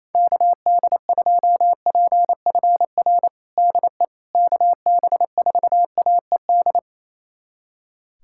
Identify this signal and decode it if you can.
Morse code, heard as a keyed tone, about 27 wpm, 700 Hz. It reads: KD2PFR DE KB4AED
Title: USA Callsign DE Callsign - 6 Characters - 28wpm